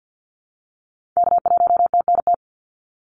Morse code, heard as a keyed tone, about 50 wpm, 700 Hz. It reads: K1TNT
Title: NA SOTA Activator Callsigns 50wpm